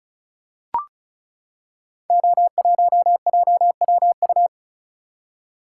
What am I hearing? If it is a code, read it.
Morse code, heard as a keyed tone, about 36 wpm, 690 Hz. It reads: O1JWU
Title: Code Groups - 5 Letter:Numbers - 35wpm